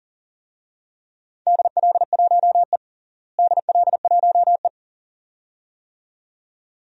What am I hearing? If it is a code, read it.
Morse code, heard as a keyed tone, about 40 wpm, 700 Hz. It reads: DL1E DL1E
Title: DXCC-100 Call Signs - 4 Characters - Repeated 2x - 40wpm